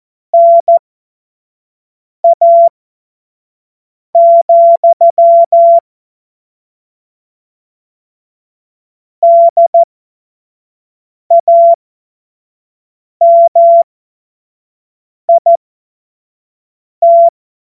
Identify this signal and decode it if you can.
Morse code, heard as a keyed tone, about 14 wpm, spaced as for 5 wpm, 680 Hz. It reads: NA, DAMIT
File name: Morsekurs / Witze / Witz 07. September 2021, 25 F70